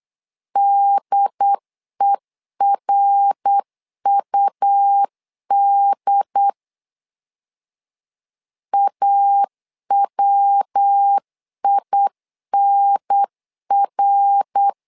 Morse code, heard as a keyed tone, about 8 wpm, 800 Hz. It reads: DERUD AWINR
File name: LEZIONI / 3 MWRO / 3 MWR 8wpm 2444